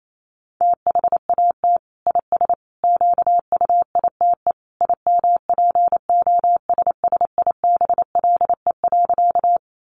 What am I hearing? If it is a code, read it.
Morse code, heard as a keyed tone, about 28 wpm, 700 Hz. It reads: THAT IS QUITE IMPOSSIBLE.
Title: Sentences from Top 1100 Words 28wpm